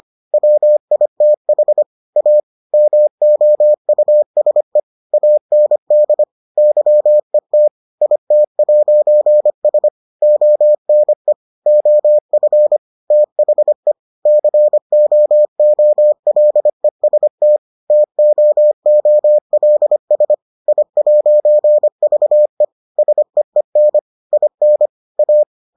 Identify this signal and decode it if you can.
Morse code, heard as a keyed tone, about 25 wpm, 600 Hz. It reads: WITH A MOUSE AND YET IT'S ONE OF THE COOLEST TOOLS I'VE SEEN IN A